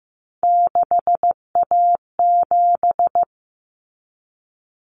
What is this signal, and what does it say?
Morse code, heard as a keyed tone, about 15 wpm, 700 Hz. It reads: 6A7
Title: Common Cut Numbers - 3-Digit 15wpm